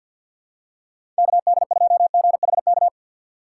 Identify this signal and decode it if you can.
Morse code, heard as a keyed tone, about 53 wpm, 700 Hz. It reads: KD2GHK
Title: USA Call Signs - 6 Characters - 4x Character Spacing - 50wpm